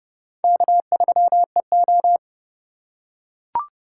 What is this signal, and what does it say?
Morse code, heard as a keyed tone, about 29 wpm, 700 Hz. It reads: K3EO